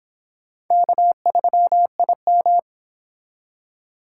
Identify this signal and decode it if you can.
Morse code, heard as a keyed tone, about 26 wpm, 700 Hz. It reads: K3IM